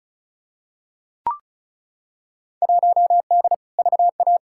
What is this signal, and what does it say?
Morse code, heard as a keyed tone, about 36 wpm, 700 Hz. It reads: 1D VA
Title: Field Day Exchanges - 35wpm